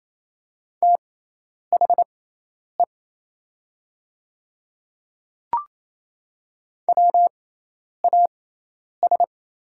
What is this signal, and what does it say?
Morse code, heard as a keyed tone, about 27 wpm, spaced as for 10 wpm, 700 Hz. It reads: THE WAS
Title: 6x Character Spacing - Top 100 Words - 28wpm